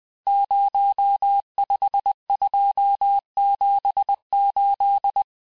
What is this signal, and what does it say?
Morse code, heard as a keyed tone, about 20 wpm, 800 Hz. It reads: 05278
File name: cours-858-texte-chiffre-vitesse-19mpm